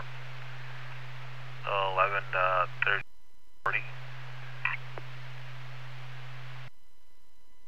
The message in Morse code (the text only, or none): none